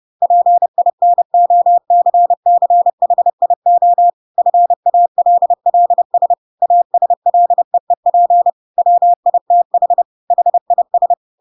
Morse code, heard as a keyed tone, about 30 wpm, 700 Hz. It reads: PINOCCHIO FALLS ASLEEP WITH HIS
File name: Pinocchio / Pinocchio ch6 30 30 ik1hge